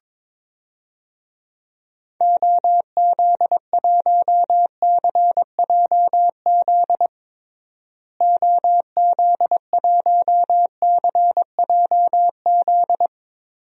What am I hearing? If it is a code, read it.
Morse code, heard as a keyed tone, about 22 wpm, 700 Hz. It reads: OZ1CJZ OZ1CJZ
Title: DXCC-100 Call Signs - 6 Characters - Repeated 3x - 22wpm